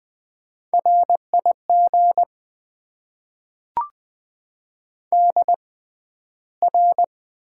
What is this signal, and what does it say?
Morse code, heard as a keyed tone, about 20 wpm, spaced as for 12 wpm, 700 Hz. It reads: RIG DR
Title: Top 90 QSO Elements - 6x Character Spacing - 20wpm